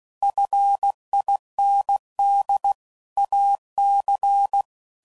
Morse code, heard as a keyed tone, about 16 wpm, 800 Hz. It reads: FIND AC
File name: CW Code Groups @15WPM/Technical Subjects